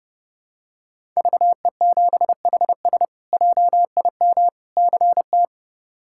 Morse code, heard as a keyed tone, about 30 wpm, 700 Hz. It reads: VE7HS JIM CT
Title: K1USN SST QSO Practice - 30wpm